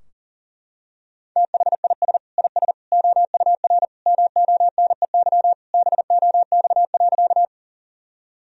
Morse code, heard as a keyed tone, about 40 wpm, 700 Hz. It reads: THIS IS OUR MONEY BOX.
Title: Sentences from Top 1000 Words - Review Part 2 - 40wpm